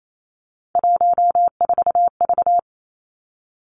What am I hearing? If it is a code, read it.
Morse code, heard as a keyed tone, about 28 wpm, 700 Hz. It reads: 14V
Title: Three Challenging Letter-Number Combinations 28wpm